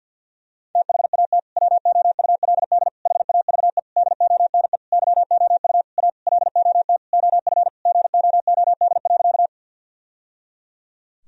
Morse code, heard as a keyed tone, about 53 wpm, 700 Hz. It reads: THAT WOULD HAVE DONE YOU A LOT OF GOOD.
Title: Sentences from Top 1000 Words - Review Part 3 - 50wpm